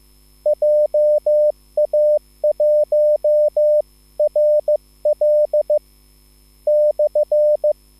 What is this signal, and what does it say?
Morse code, heard as a keyed tone, about 15 wpm, spaced as for 12 wpm, 600 Hz. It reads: JA1RL/
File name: QSL / CW lesson